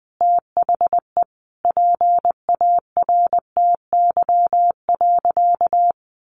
Morse code, heard as a keyed tone, about 20 wpm, 700 Hz. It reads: THE PARTY.
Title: Sentences from Top 1400 Words 20wpm